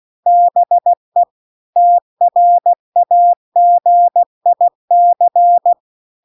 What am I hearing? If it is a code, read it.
Morse code, heard as a keyed tone, about 16 wpm, 700 Hz. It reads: BE TRAGIC
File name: CW-The-Big-Twoheaded-River-2-16WPM